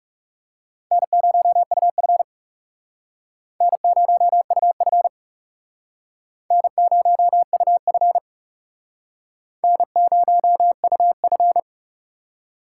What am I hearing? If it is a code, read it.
Morse code, heard as a keyed tone, about 38 wpm, 700 Hz. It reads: N0UF N0UF N0UF N0UF